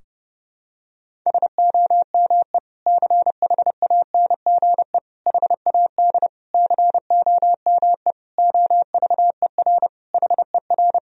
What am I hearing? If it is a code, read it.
Morse code, heard as a keyed tone, about 30 wpm, 700 Hz. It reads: SOME CHANGE HAD COME OVER HER
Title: Sentences from Top 1000 Words - Review Part 2 - 30wpm